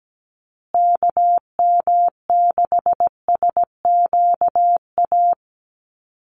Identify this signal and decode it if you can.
Morse code, heard as a keyed tone, about 17 wpm, 700 Hz. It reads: KM6SQA
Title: USA Call Signs Repeated 2x 17wpm